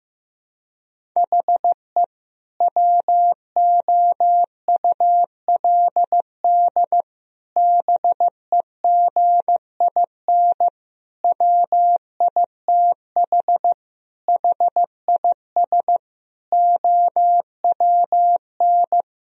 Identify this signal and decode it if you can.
Morse code, heard as a keyed tone, about 15 wpm, 700 Hz. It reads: HE WOULD BEGIN WITH HIS OWN
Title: Sentences from Top 1000 Words - Review Part 1 - 15wpm